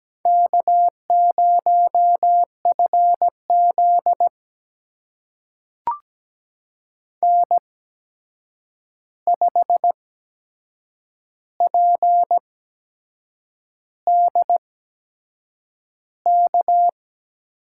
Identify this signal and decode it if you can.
Morse code, heard as a keyed tone, about 17 wpm, spaced as for 7 wpm, 700 Hz. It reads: K0FZ N5PDK
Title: USA Call Signs - 8x Character Spacing - 17wpm